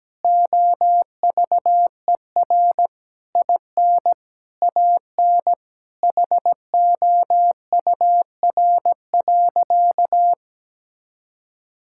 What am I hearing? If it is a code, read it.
Morse code, heard as a keyed tone, about 17 wpm, 700 Hz. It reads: OVER IN AN HOUR.
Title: Sentences from Top 1000 Words - Review Part 5 - 17wpm